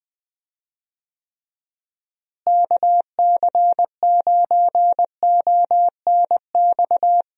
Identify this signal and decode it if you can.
Morse code, heard as a keyed tone, about 21 wpm, 700 Hz. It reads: KC9ONX
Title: USA Call Signs - 6 Characters - Repeated 3x - 20wpm